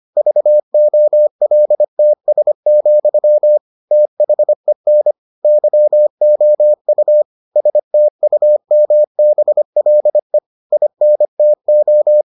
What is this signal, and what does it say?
Morse code, heard as a keyed tone, about 25 wpm, 600 Hz. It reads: VOLTS, THEN YOU STUMBLE INTO